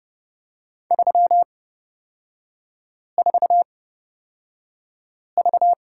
Morse code, heard as a keyed tone, about 29 wpm, spaced as for 5 wpm, 700 Hz. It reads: 34V
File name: Lesson 38e - Letter q - Three Characters 30wpm